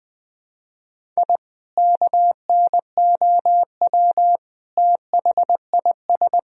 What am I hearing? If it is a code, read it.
Morse code, heard as a keyed tone, about 20 wpm, 700 Hz. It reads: I KNOW THIS
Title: Sentences from Top 1000 Words - Review Part 2 - 20wpm